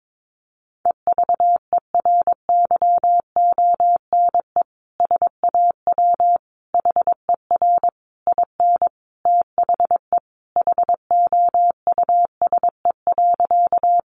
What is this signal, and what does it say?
Morse code, heard as a keyed tone, about 22 wpm, 700 Hz. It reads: EVERYONE SAW HER IN THE HOUSE.
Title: Sentences from Top 800 Words 22wpm